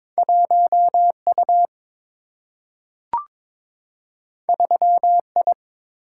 Morse code, heard as a keyed tone, about 22 wpm, 700 Hz. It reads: 1 U 3 I
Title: Single-Letter Single-Number 22wpm